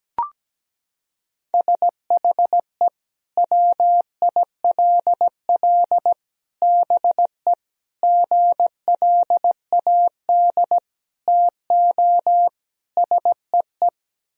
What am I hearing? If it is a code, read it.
Morse code, heard as a keyed tone, about 17 wpm, 700 Hz. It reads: SHE WILL BE GLAD TO SEE
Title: Sentences from Top 1000 Words - Review Part 2 - 17wpm